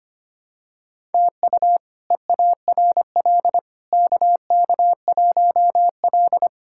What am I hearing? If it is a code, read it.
Morse code, heard as a keyed tone, about 25 wpm, 700 Hz. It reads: TU EARL KK1L
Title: K1USN SST QSO Practice - 25wpm